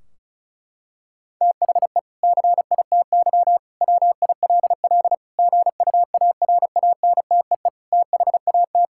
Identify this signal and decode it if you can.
Morse code, heard as a keyed tone, about 34 wpm, 700 Hz. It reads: THE CITY WILL GUARANTEE THAT
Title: Sentences from Top 1600 Words 35wpm